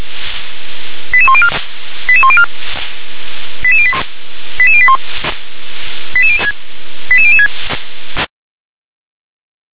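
Five-tone selective call (ZVEI), heard as quick runs of five tones, tones of about 70 ms. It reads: 88104 88104 88081 88006
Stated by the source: SDR-received ZVEI five-tone selective call, code 88005.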